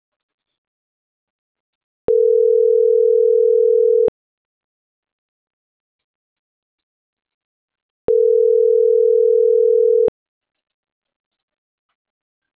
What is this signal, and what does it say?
Ringback tone (North America), 440 + 480 Hz.